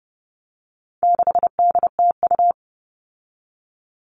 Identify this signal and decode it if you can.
Morse code, heard as a keyed tone, about 30 wpm, 700 Hz. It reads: 6DTU
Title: All Cut Numbers - 4-Digit 30wpm